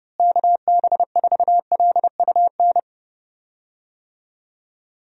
Morse code, heard as a keyed tone, about 30 wpm, 700 Hz. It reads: KB4LUN
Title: USA Call Signs - 6 Characters - Repeated 2x - 30wpm